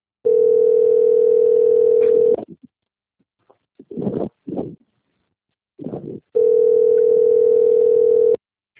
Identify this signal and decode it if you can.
Ringback tone (North America), 440 + 480 Hz.
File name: out- - - - - .11681